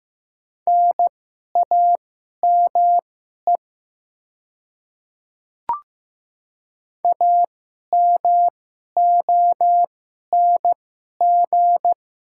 Morse code, heard as a keyed tone, about 15 wpm, spaced as for 11 wpm, 700 Hz. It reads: NAME AMONG
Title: 2x Character Spacing - Top 500 Words - 15wpm